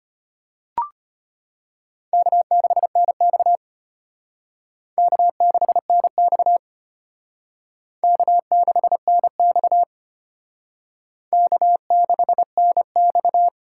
Morse code, heard as a keyed tone, about 32 wpm, 700 Hz. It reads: K6NX K6NX K6NX K6NX